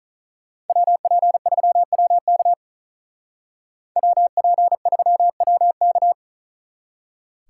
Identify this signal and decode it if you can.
Morse code, heard as a keyed tone, about 43 wpm, 700 Hz. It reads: WP3WK WP3WK
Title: Speed Racing - USA 2x2 Call Signs 35wpm